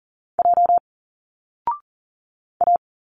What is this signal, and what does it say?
Morse code, heard as a keyed tone, about 40 wpm, 700 Hz. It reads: J A
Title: Single Letters 40wpm